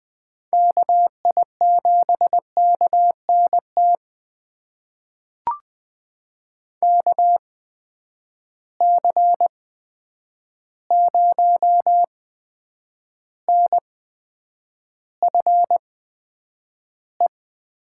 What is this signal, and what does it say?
Morse code, heard as a keyed tone, about 20 wpm, spaced as for 9 wpm, 700 Hz. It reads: KI7KNT KC0NFE
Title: USA Call Signs - 6 Characters - 8x Character Spacing - 20wpm